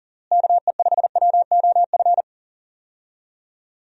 Morse code, heard as a keyed tone, about 40 wpm, 700 Hz. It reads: KE5WOF
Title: USA Call Signs - 6 Characters - Repeated 3x - 40wpm